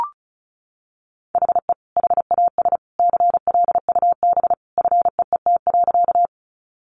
Morse code, heard as a keyed tone, about 34 wpm, 700 Hz. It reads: HE HAS CLUB FEET.
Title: Sentences from Top 1200 Words 35wpm